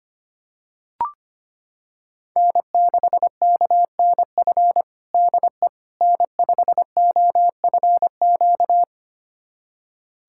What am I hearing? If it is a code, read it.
Morse code, heard as a keyed tone, about 25 wpm, 700 Hz. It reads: N6KNF DE N5OFQ
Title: USA Callsign DE Callsign - 5 Characters - 25wpm